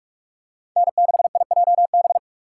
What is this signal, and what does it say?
Morse code, heard as a keyed tone, about 39 wpm, 700 Hz. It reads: N6IJB